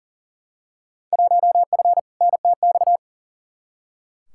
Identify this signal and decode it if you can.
Morse code, heard as a keyed tone, about 40 wpm, 700 Hz. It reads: 1F NTX
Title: Field Day Exchanges Encore - 40wpm